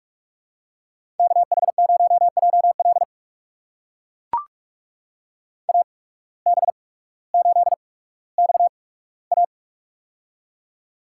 Morse code, heard as a keyed tone, about 42 wpm, 690 Hz. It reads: K H 0 J L A B 8 X A
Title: USA Call Signs - 8x Character Spacing - 45wpm